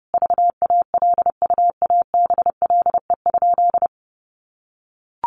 Morse code, heard as a keyed tone, about 29 wpm, 700 Hz. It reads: VALUABLE?
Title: Sentences from Top 1100 Words 30wpm